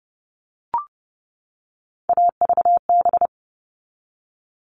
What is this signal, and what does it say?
Morse code, heard as a keyed tone, about 30 wpm, 700 Hz. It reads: AVB
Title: All Cut Numbers - 3-Digit 30wpm